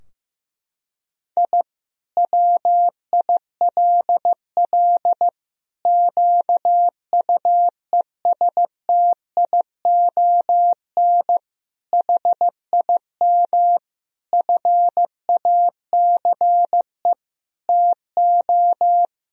Morse code, heard as a keyed tone, about 15 wpm, 700 Hz. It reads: I WILL QUESTION HIM FACE TO
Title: Sentences from Top 1000 Words - Review Part 2 - 15wpm